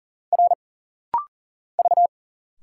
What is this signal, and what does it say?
Morse code, heard as a keyed tone, about 40 wpm, 700 Hz. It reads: R V
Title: Single Letters - Rapid-Fire 40wpm